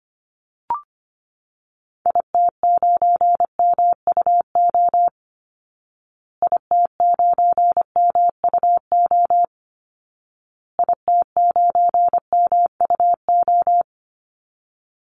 Morse code, heard as a keyed tone, about 25 wpm, 700 Hz. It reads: IT9MUO IT9MUO IT9MUO
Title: Call Signs Repeated 2x - 25wpm